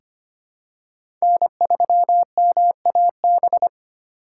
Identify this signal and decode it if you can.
Morse code, heard as a keyed tone, about 25 wpm, 700 Hz. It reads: N3MAB
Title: USA Call Signs - 4x Character Spacing - 25wpm